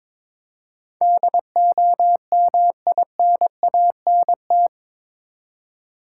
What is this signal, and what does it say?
Morse code, heard as a keyed tone, about 22 wpm, 700 Hz. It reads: DOMINANT
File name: practice / 22wpm / Eight Letter Words - 22wpm